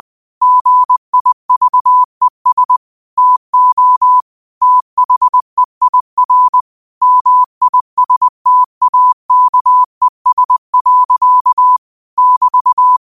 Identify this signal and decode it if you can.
Morse code, heard as a keyed tone, about 20 wpm, 1000 Hz. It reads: GIVES TO THEIR MISTAKES. <BT>
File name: qotdfiles / 20wpm